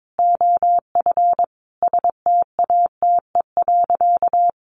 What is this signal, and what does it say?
Morse code, heard as a keyed tone, about 22 wpm, 700 Hz. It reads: OF STATE.
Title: Sentences from Top 1300 Words 22wpm